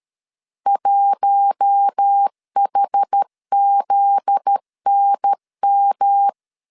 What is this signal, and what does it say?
Morse code, heard as a keyed tone, about 13 wpm, 800 Hz. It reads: 1HZNM
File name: LEZIONI / 7 GF9Z / 7 GF9Z 12wpm 2147